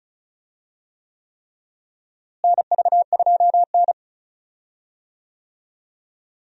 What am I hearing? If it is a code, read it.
Morse code, heard as a keyed tone, about 36 wpm, 700 Hz. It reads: NV2N
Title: USA Call Signs - 4 Characters - Repeated 3x - 35wpm